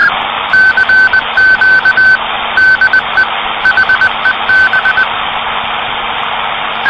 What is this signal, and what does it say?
Morse code, heard as a keyed tone, about 20 wpm, 1550 Hz, with noise. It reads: CQ DE HEB
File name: CW PACTOR CQ de HEB